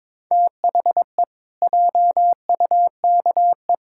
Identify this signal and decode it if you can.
Morse code, heard as a keyed tone, about 22 wpm, 700 Hz. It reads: THE JUKE
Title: Pangram Challenge v2 - 22wpm